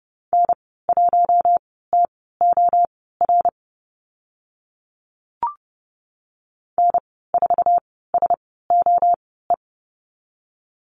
Morse code, heard as a keyed tone, about 30 wpm, spaced as for 17 wpm, 700 Hz. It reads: N1TOR N4SOE
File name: Lesson 14g - Number 2 - Call Signs 30wpm